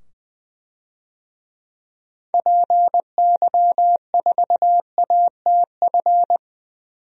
Morse code, heard as a keyed tone, about 20 wpm, 700 Hz. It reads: PY4ATF
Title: DXCC-100 Call Signs - 6 Characters - 20wpm